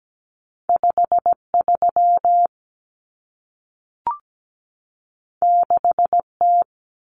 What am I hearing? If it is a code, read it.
Morse code, heard as a keyed tone, about 17 wpm, 700 Hz. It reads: 53 6T
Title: Common Cut Numbers - 2-Digit 17wpm